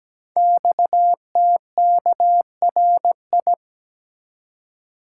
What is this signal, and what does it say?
Morse code, heard as a keyed tone, about 17 wpm, 700 Hz. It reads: XTKRI
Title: Code Groups - 5 Letters - 17wpm